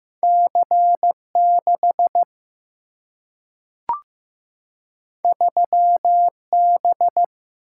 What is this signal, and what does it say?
Morse code, heard as a keyed tone, about 15 wpm, 700 Hz. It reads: C6 3B
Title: Single-Letter Single-Number 15wpm